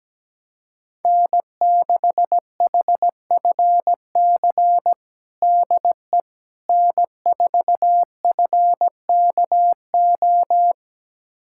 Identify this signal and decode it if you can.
Morse code, heard as a keyed tone, about 17 wpm, 700 Hz. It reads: N6HFC DE N4FKO
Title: USA Callsign DE Callsign - 5 Characters - 17wpm